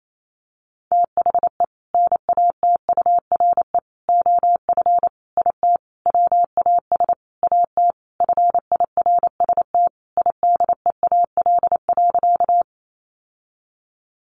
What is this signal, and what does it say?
Morse code, heard as a keyed tone, about 28 wpm, 700 Hz. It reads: THE NATURE OF IT WAS AT FIRST IDEAL.
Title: Sentences from Top 1400 Words 28wpm